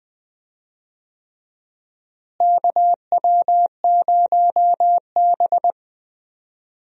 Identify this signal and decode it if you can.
Morse code, heard as a keyed tone, about 20 wpm, 700 Hz. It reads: KW0B
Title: USA Call Signs - 4 Characters - Repeated 3x - 20wpm